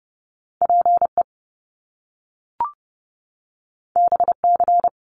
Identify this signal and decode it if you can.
Morse code, heard as a keyed tone, about 30 wpm, 700 Hz. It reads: PE BC
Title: Canadian Provinces and Territories 30wpm